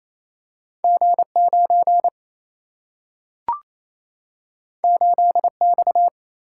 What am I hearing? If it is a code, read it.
Morse code, heard as a keyed tone, about 28 wpm, 700 Hz. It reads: G 9 8 X
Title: Single-Letter Single-Number 28wpm